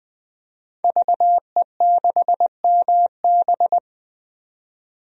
Morse code, heard as a keyed tone, about 20 wpm, 700 Hz. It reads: VE6MB
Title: DXCC-100 Call Signs - 5 Characters - 20wpm